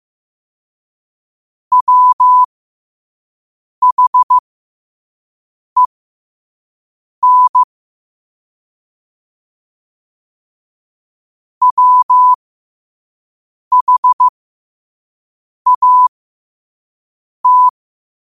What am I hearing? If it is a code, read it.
Morse code, heard as a keyed tone, about 15 wpm, spaced as for 5 wpm, 1000 Hz. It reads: WHEN WHAT
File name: qotdfiles / 5wpm